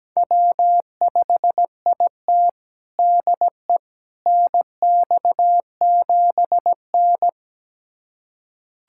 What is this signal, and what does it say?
Morse code, heard as a keyed tone, about 17 wpm, 700 Hz. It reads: W5IT DE NX7N
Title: USA Callsign DE Callsign - 4 Characters - 17wpm